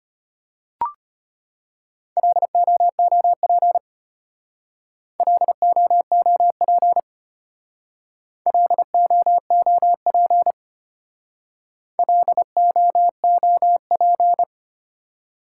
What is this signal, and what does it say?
Morse code, heard as a keyed tone, about 32 wpm, 700 Hz. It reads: LOOP LOOP LOOP LOOP